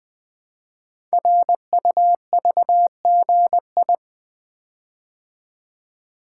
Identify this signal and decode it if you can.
Morse code, heard as a keyed tone, about 20 wpm, 700 Hz. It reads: RUVGI
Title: Code Groups - 5 Letter:Numbers - 20wpm